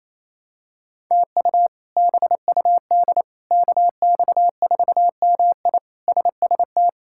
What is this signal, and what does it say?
Morse code, heard as a keyed tone, about 27 wpm, 700 Hz. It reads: TU BUD KX4MI SST
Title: K1USN SST QSO Practice - 28wpm